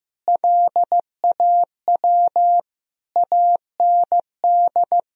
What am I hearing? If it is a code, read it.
Morse code, heard as a keyed tone, about 15 wpm, 700 Hz. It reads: LAW AND
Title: Sentences from Top 1000 Words - Review Part 3 - 15wpm